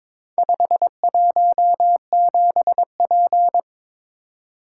Morse code, heard as a keyed tone, about 22 wpm, 690 Hz. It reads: 517P
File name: practice / 22wpm / Four Letter-Number Combinations - 22wpm